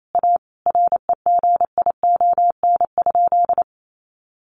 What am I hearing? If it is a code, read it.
Morse code, heard as a keyed tone, about 28 wpm, 700 Hz. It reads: A REGION?
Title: Sentences from Top 1500 Words 28wpm